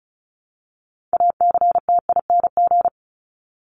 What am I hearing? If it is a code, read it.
Morse code, heard as a keyed tone, about 34 wpm, 700 Hz. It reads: ACTING
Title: ING Suffix - Top 100 Words with Base Words 35wpm